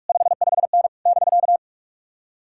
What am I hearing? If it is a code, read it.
Morse code, heard as a keyed tone, about 59 wpm, 700 Hz. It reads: 55N <BK>